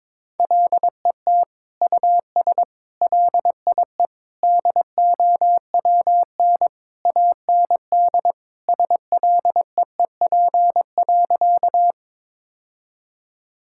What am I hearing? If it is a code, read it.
Morse code, heard as a keyed tone, about 22 wpm, 700 Hz. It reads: LET US LIE DOWN AND SLEEP.
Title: Sentences from Top 1600 Words 22wpm